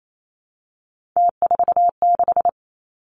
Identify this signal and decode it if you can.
Morse code, heard as a keyed tone, about 28 wpm, 700 Hz. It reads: T46
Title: Three Challenging Letter-Number Combinations 28wpm